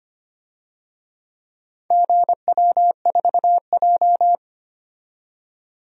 Morse code, heard as a keyed tone, about 25 wpm, 700 Hz. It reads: GW4J
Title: DXCC-100 Call Signs - 4 Characters - 25wpm